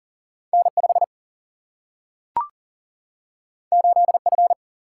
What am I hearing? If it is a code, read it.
Morse code, heard as a keyed tone, about 38 wpm, 700 Hz. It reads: N 5 8 F
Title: Single-Letter Single-Number 40wpm